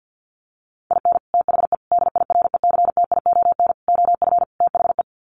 Morse code, heard as a keyed tone, about 50 wpm, 700 Hz. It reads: IN THE DIRECTION OF THE
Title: Sets of 5 Words 50wpm